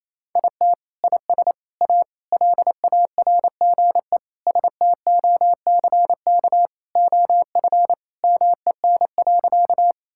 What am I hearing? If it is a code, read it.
Morse code, heard as a keyed tone, about 28 wpm, 700 Hz. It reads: IT IS A LARGE STOCK OF MEN.
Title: Sentences from Top 1000 Words - Review Part 5 - 28wpm